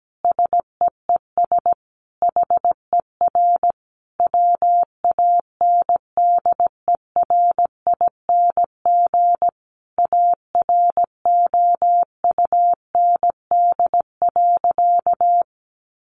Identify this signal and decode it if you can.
Morse code, heard as a keyed tone, about 17 wpm, 700 Hz. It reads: SEES HER WANDERING AROUND.
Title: Additional Easy Phrases 17wpm